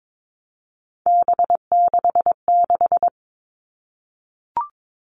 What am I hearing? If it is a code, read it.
Morse code, heard as a keyed tone, about 22 wpm, 700 Hz. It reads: B66
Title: All Cut Numbers - 3-Digit 22wpm